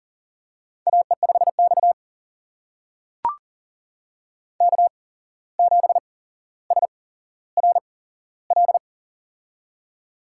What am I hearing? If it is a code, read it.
Morse code, heard as a keyed tone, about 40 wpm, spaced as for 16 wpm, 700 Hz. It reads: AE5X K7SRL
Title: USA Call Signs - 8x Character Spacing - 40wpm